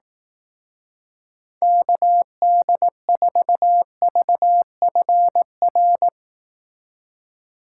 Morse code, heard as a keyed tone, about 18 wpm, 700 Hz. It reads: KD4VFR
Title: Speed Racing - USA 2x3 Call Signs 15wpm